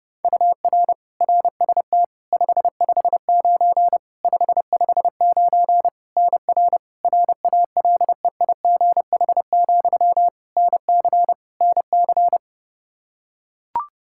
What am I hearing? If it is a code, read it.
Morse code, heard as a keyed tone, about 30 wpm, 700 Hz. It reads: UR RST 559 559 NR RALEIGH, NC NC
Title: K1USN SST QSO Practice 30wpm